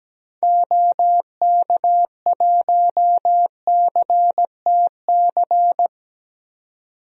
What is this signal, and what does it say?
Morse code, heard as a keyed tone, about 17 wpm, 700 Hz. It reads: OK1CTC
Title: DXCC-100 Call Signs - 6 Characters - 17wpm